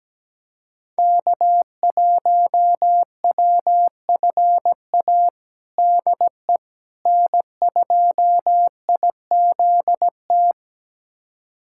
Morse code, heard as a keyed tone, about 17 wpm, 700 Hz. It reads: K1WFA DE N2IZT
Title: USA Callsign DE Callsign - 5 Characters - 17wpm